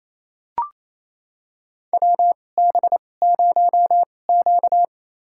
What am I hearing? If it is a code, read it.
Morse code, heard as a keyed tone, about 28 wpm, spaced as for 20 wpm, 700 Hz. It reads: WB0Q
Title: USA Call Signs - 4 Characters - 2x Character Spacing - 28wpm